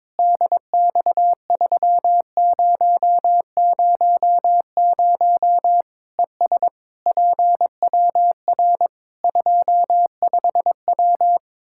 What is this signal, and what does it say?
Morse code, heard as a keyed tone, about 22 wpm, 700 Hz. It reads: DX3000 ES PWR 25W
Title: K1USN SST QSO Practice 22wpm